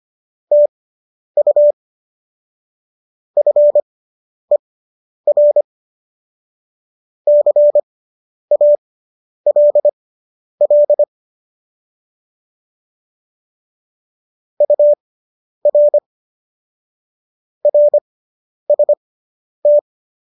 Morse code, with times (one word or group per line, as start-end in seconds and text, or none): 0.5-1.7 TU
3.4-5.6 FER
7.3-11.0 CALL
14.6-16.0 UR
17.6-19.8 RST